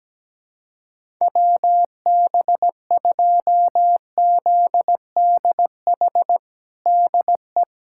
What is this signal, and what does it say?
Morse code, heard as a keyed tone, about 17 wpm, 700 Hz. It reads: WB2ZDH DE
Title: USA Callsign DE Callsign 17wpm